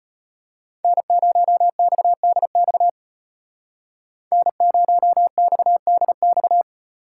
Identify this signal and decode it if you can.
Morse code, heard as a keyed tone, about 33 wpm, 700 Hz. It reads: N0XDX N0XDX